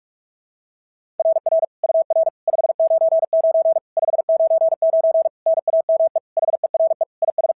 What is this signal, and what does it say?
Morse code, heard as a keyed tone, about 41 wpm, 660 Hz. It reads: RR UR 599 599 NAME HERE IS
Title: SOTA QSO Practice 45wpm